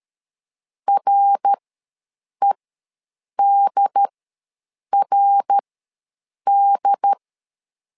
Morse code, heard as a keyed tone, about 13 wpm, spaced as for 7 wpm, 800 Hz. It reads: REDRD
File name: LEZIONI / 3 MWRO / 3 MWR ws 12wpm 1809